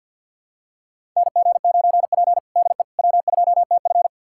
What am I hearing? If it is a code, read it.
Morse code, heard as a keyed tone, about 46 wpm, 700 Hz. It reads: NG9P DE W2TF
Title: USA Callsign DE Callsign - 4 Characters - 50wpm